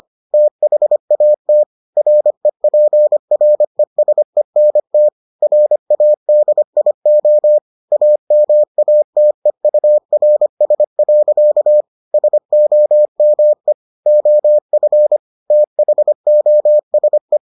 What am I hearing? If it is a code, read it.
Morse code, heard as a keyed tone, about 25 wpm, 600 Hz. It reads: THAT REPRESENT RADIO AMATEURS. SOME OF THOSE